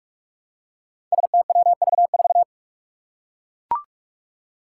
Morse code, heard as a keyed tone, about 44 wpm, 700 Hz. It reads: STWV4
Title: Five Challenging Letter-Number Combinations 45wpm